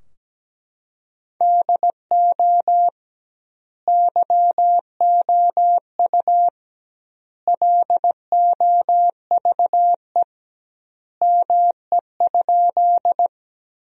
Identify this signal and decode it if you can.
Morse code, heard as a keyed tone, about 17 wpm, 700 Hz. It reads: DO YOU LOVE ME?
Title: 2x Word-Spacing - Sentences from Top 400 Words - 17wpm